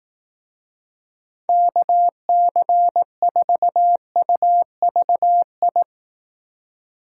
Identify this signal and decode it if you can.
Morse code, heard as a keyed tone, about 18 wpm, 700 Hz. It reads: KC4UVI
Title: Speed Racing - USA Call Signs 15wpm